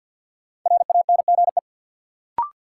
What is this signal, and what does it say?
Morse code, heard as a keyed tone, about 53 wpm, 700 Hz. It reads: RANGE
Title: Sentences from Top 900 Words - New Word Drill - 50wpm